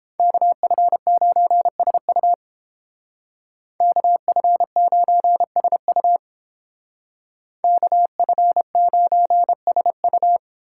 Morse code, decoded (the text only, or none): KF9SU KF9SU KF9SU